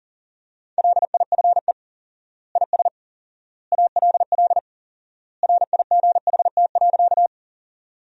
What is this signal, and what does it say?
Morse code, heard as a keyed tone, about 40 wpm, 700 Hz. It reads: LIFE IS ALL RIGHT.
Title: 4x Word-Spacing - Sentences from Top 300 Words - 40wpm